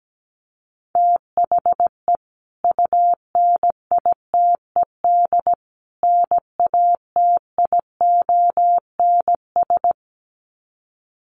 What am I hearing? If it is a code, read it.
Morse code, heard as a keyed tone, about 17 wpm, 700 Hz. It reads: THE UNITED NATIONS
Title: Sets of 3 Words 17wpm